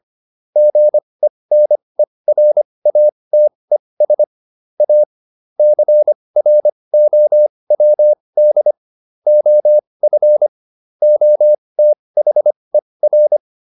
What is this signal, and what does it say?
Morse code, heard as a keyed tone, about 25 wpm, spaced as for 20 wpm, 600 Hz. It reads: GENERATES A CROWD OF OTHER